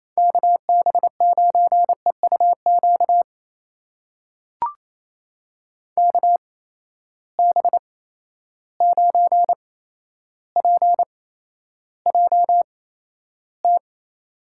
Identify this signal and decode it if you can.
Morse code, heard as a keyed tone, about 28 wpm, spaced as for 12 wpm, 700 Hz. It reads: KB9EUQ KB9PJT
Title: USA Call Signs - 6 Characters - 8x Character Spacing - 28wpm